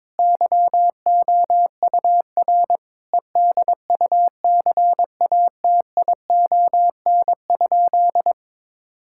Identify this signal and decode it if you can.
Morse code, heard as a keyed tone, about 22 wpm, 700 Hz. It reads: YOUR EDUCATION?
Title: Sentences from Top 1000 Words - Review Part 6 - 22wpm